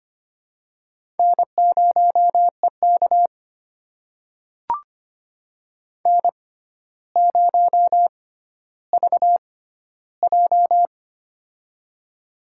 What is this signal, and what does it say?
Morse code, heard as a keyed tone, about 25 wpm, spaced as for 13 wpm, 700 Hz. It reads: N0EK N0VJ
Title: USA Call Signs - 4 Characters - 6x Character Spacing - 25wpm